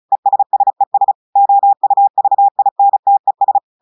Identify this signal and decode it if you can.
Morse code, heard as a keyed tone, about 36 wpm, 800 Hz. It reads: ESSES OUVINTES
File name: portugais texte en clair 2 vitesse 35 mots minutes